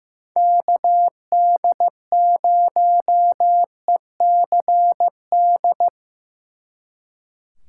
Morse code, heard as a keyed tone, about 18 wpm, 700 Hz. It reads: KD0ECD
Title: Speed Racing - USA 2x3 Call Signs 15wpm